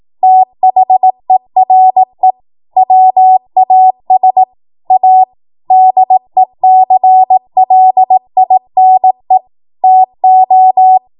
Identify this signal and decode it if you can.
Morse code, heard as a keyed tone, about 18 wpm, 750 Hz. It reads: THERE WAS A DECLINE TO
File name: Morse / 18WPM